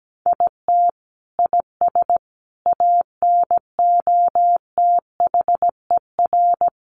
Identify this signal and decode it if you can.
Morse code, heard as a keyed tone, about 17 wpm, 700 Hz. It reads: IT IS ANOTHER
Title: Sentences from Top 1400 Words 17wpm